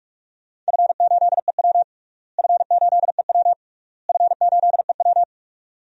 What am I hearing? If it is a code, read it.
Morse code, heard as a keyed tone, about 41 wpm, 700 Hz. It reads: F8EW F8EW F8EW
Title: DXCC-100 Call Signs - 4 Characters - Repeated 3x - 45wpm